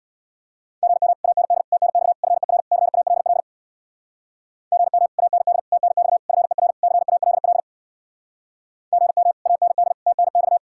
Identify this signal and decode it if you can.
Morse code, heard as a keyed tone, about 64 wpm, 700 Hz. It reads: GM AND TNX FER CALL GM AND TNX FER CALL GM AND TNX